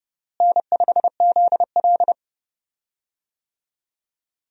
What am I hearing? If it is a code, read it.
Morse code, heard as a keyed tone, about 31 wpm, 700 Hz. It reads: N5ZL
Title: USA Call Signs - 4 Characters - Repeated 2x - 30wpm